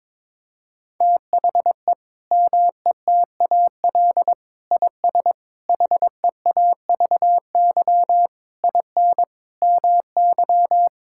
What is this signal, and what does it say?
Morse code, heard as a keyed tone, about 22 wpm, 700 Hz. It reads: THE METAL IS HEAVY IN MY
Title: Sentences from Top 1000 Words - Review Part 6 - 22wpm